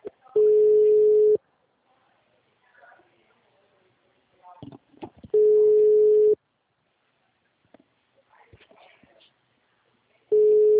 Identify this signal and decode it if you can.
Ringback tone (Europe, 425 Hz), with speech.